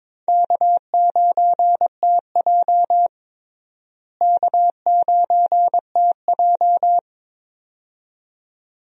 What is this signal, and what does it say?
Morse code, heard as a keyed tone, about 22 wpm, 700 Hz. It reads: K9TJ K9TJ
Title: USA Call Signs - 4 Characters - Repeated 2x - 22wpm